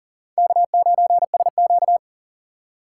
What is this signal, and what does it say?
Morse code, heard as a keyed tone, about 40 wpm, 700 Hz. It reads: K9SQ
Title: USA Call Signs - 4 Characters - 6x Character Spacing - 40wpm